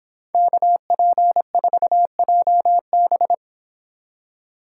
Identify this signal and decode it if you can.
Morse code, heard as a keyed tone, about 26 wpm, 700 Hz. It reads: KP4JB
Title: Speed Racing - USA 2x2 Call Signs 22wpm